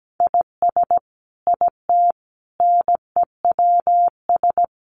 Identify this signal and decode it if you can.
Morse code, heard as a keyed tone, about 17 wpm, 700 Hz. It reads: IS IT NEWS
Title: Sentences from Top 800 Words 17wpm